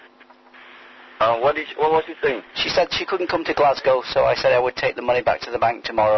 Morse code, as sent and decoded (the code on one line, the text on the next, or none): none